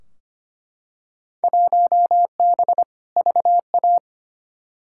Morse code, heard as a keyed tone, about 25 wpm, 700 Hz. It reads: 1B VA
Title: Field Day Exchanges Encore - 25wpm